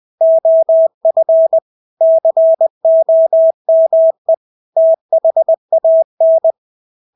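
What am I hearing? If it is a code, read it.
Morse code, heard as a keyed tone, about 20 wpm, 650 Hz. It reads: OF COME THAN
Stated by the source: Morse practice of English words at 20 wpm.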